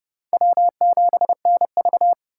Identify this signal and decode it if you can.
Morse code, heard as a keyed tone, about 30 wpm, 700 Hz. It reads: W7NV
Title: USA Call Signs - 4 Characters - 2x Character Spacing - 30wpm